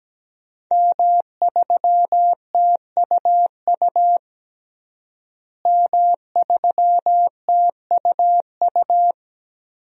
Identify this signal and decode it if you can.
Morse code, heard as a keyed tone, about 17 wpm, 700 Hz. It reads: M3TUU M3TUU
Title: DXCC-100 Call Signs - 5 Characters - Repeated 2x - 17wpm